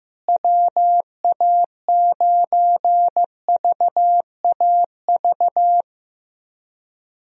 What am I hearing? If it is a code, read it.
Morse code, heard as a keyed tone, about 15 wpm, 700 Hz. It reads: WA9VAV
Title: USA Call Signs - 6 Characters - Repeated 3x - 15wpm